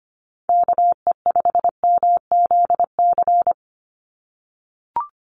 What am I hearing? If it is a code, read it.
Morse code, heard as a keyed tone, about 25 wpm, 700 Hz. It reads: KE5MZC
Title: USA Call Signs 25wpm